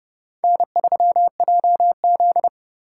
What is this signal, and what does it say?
Morse code, heard as a keyed tone, about 38 wpm, 700 Hz. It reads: N3JZ